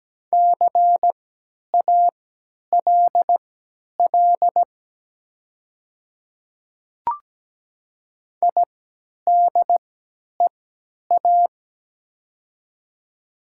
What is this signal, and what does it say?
Morse code, heard as a keyed tone, about 17 wpm, spaced as for 10 wpm, 700 Hz. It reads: CALL IDEA
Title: 3x Character Spacing - Top 300 Words - 17wpm